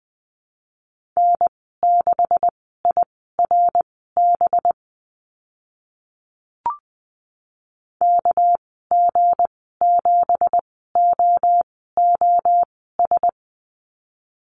Morse code, heard as a keyed tone, about 20 wpm, spaced as for 14 wpm, 700 Hz. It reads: N6IRB KG7OOS
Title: 245 Lesson 39g - Letter z - Call Signs 20wpm